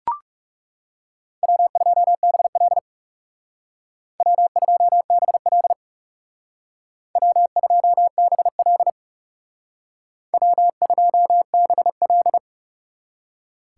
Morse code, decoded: W2BL W2BL W2BL W2BL

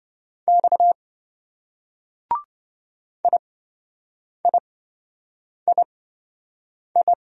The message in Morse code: X IIII

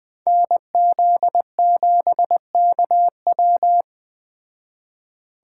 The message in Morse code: NZ7KW

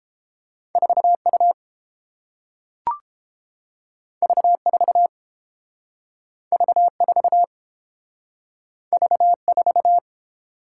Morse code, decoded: 4U V4 V4 V4

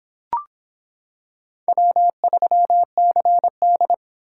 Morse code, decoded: W3CD

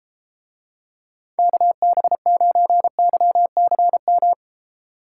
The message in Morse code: KB9YCM

